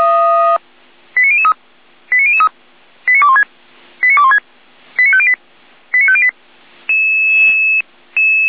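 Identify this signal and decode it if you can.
Five-tone selective call (ZVEI), heard as quick runs of five tones, tones of about 80 ms. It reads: 89003 89003 89216 89216 89508 89508
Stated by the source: Five-tone ZVEI selective call 89014, received on SDR.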